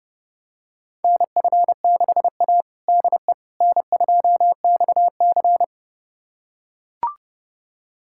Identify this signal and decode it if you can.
Morse code, heard as a keyed tone, about 32 wpm, 700 Hz. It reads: NF6A DE N2XC
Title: USA Callsign DE Callsign - 4 Characters - 30wpm